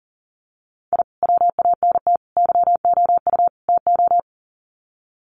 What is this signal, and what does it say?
Morse code, heard as a keyed tone, about 40 wpm, 700 Hz. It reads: I WANT YOU TO
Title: Sets of 4 Words 40wpm